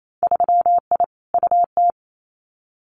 Morse code, heard as a keyed tone, about 28 wpm, 700 Hz. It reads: 3I UT
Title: Winter Field Day Exchanges Encore 28wpm